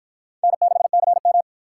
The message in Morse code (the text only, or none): N6CM